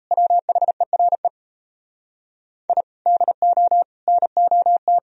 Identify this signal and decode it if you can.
Morse code, heard as a keyed tone, about 38 wpm, 700 Hz. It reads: WHERE I DO NOT